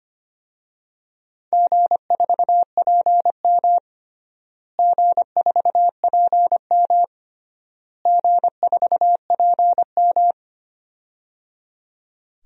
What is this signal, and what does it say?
Morse code, heard as a keyed tone, about 25 wpm, 700 Hz. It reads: G4PM G4PM G4PM
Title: DXCC-100 Call Signs - 4 Characters - Repeated 3x - 25wpm